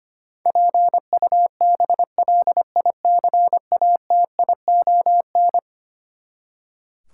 Morse code, heard as a keyed tone, about 25 wpm, 700 Hz. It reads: PUBLICATION
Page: morse-code-ninja-practice-25wpm